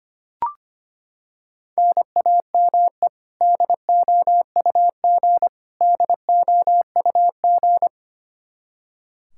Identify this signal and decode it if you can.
Morse code, heard as a keyed tone, about 25 wpm, 700 Hz. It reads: NAME DOUG DOUG
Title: Practice QSO - Exchange 1 - Beginning - 25wpm